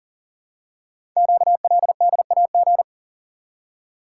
Morse code, heard as a keyed tone, about 42 wpm, 700 Hz. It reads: QLDAG
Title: Code Groups - 5 Letters - 40wpm